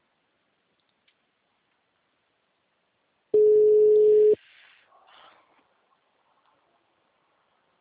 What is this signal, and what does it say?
Ringback tone (Europe, 425 Hz).
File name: out- -2003- - - .25265